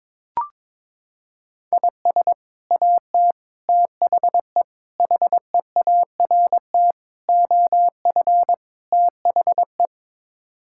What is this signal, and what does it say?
Morse code, heard as a keyed tone, about 22 wpm, 700 Hz. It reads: IS AT THE HEART OF THE
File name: practice / 22wpm / Sets of 6 Words - 22wpm